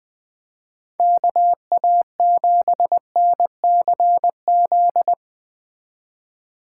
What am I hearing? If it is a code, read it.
Morse code, heard as a keyed tone, about 21 wpm, 700 Hz. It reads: KA7NCZ